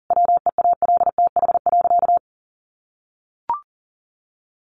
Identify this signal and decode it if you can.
Morse code, heard as a keyed tone, about 40 wpm, 700 Hz. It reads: WEALTH.
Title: Sentences from Top 1500 Words 40wpm